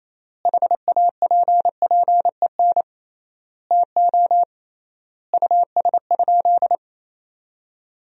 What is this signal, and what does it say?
Morse code, heard as a keyed tone, about 28 wpm, 700 Hz. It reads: HAPPEN TO US?
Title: 3x Word-Spacing - Sentences from Top 400 Words - 28wpm